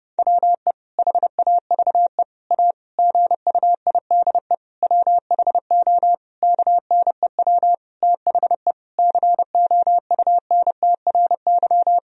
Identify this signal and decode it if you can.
Morse code, heard as a keyed tone, about 30 wpm, 700 Hz. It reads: WE HAVE A GUIDE WHO KNEW THE COUNTRY
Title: Sentences from Top 1000 Words - Review Part 3 - 30wpm